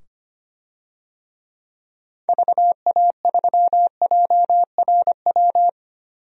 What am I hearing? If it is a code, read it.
Morse code, heard as a keyed tone, about 25 wpm, 700 Hz. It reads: VA3JRW
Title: DXCC-100 Call Signs - 6 Characters - 25wpm